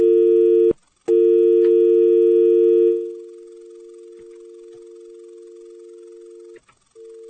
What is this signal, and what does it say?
Dial tone (UK), 350 + 450 Hz.